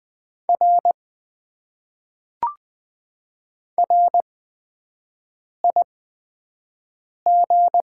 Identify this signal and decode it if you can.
Morse code, heard as a keyed tone, about 20 wpm, spaced as for 5 wpm, 700 Hz. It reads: R RIG
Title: Top 90 QSO Elements - 8x Character Spacing - 20wpm